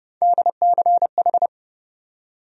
Morse code, heard as a keyed tone, about 29 wpm, 700 Hz. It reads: DCH
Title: USA Call Sign Postfixes - 3 Letters 30wpm